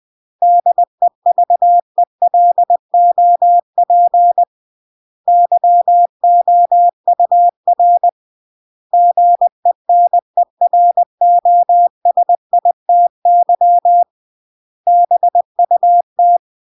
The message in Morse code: DEVELOP YOUR GENEROSITY BUT